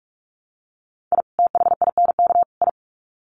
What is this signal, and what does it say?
Morse code, heard as a keyed tone, about 44 wpm, 700 Hz. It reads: I THINK I
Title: Sets of 3 Words 45wpm